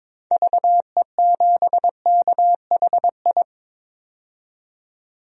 Morse code, heard as a keyed tone, about 22 wpm, 700 Hz. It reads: VE7KHI
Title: DXCC-100 Call Signs - 6 Characters - 22wpm